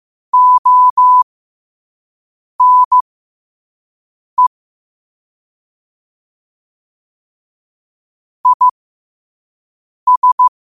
Morse code, heard as a keyed tone, about 15 wpm, spaced as for 5 wpm, 1000 Hz. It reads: ONE IS
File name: qotdfiles / 5wpm